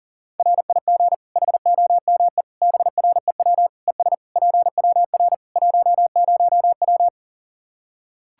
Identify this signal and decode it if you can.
Morse code, heard as a keyed tone, about 40 wpm, 700 Hz. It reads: RIG HOME BREW ES PWR 10W
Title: Practice QSO - Exchange 2 - Info - 40wpm